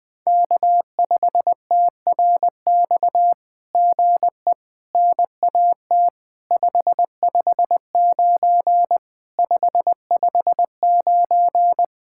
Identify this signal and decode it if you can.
Morse code, heard as a keyed tone, about 18 wpm, 700 Hz. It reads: K5TRX GE NAT 559 559
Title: SOTA QSO Practice 20wpm